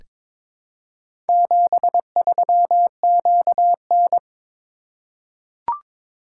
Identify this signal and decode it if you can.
Morse code, heard as a keyed tone, about 22 wpm, 700 Hz. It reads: 73QN
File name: practice / 22wpm / Four Letter-Number Combinations - 22wpm